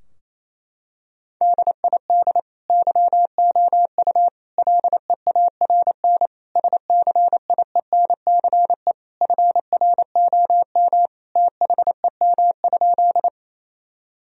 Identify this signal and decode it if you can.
Morse code, heard as a keyed tone, about 28 wpm, 700 Hz. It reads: DID YOU LEARN SCIENCE FROM THEM?
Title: Sentences from Top 1000 Words - Review Part 2 - 28wpm